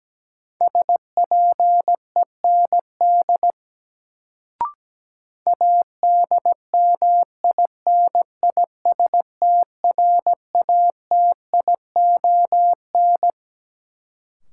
Morse code, heard as a keyed tone, about 17 wpm, 700 Hz. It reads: SPEND ADMINISTRATION
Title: Sets of 5 Words - New Word Drill - 17wpm